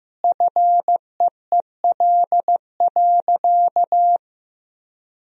Morse code, heard as a keyed tone, about 15 wpm, 700 Hz. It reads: FEEL.
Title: Sentences from Top 1000 Words - Review Part 3 - 15wpm